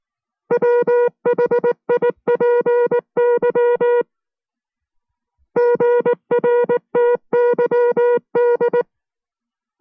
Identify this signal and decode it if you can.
Morse code, heard as a keyed tone, about 19 wpm, 480 Hz. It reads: WHIPY GRTYD